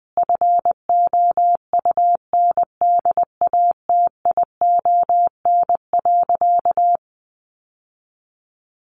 Morse code, heard as a keyed tone, about 20 wpm, 700 Hz. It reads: FOUNDATION.
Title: Sentences from Top 1200 Words 20wpm